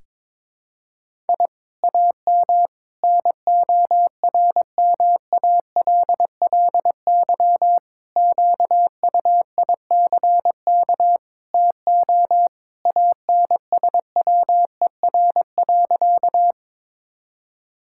Morse code, heard as a keyed tone, about 22 wpm, 700 Hz. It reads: I AM NORMALLY QUICK TO ANSWER.
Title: Sentences from Top 1000 Words - Review Part 6 - 22wpm